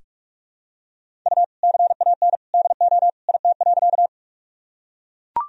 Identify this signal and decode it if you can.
Morse code, heard as a keyed tone, about 41 wpm, 700 Hz. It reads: U CAN DO IT.
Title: Sentences from Top 100 Words and Abbreviations 45wpm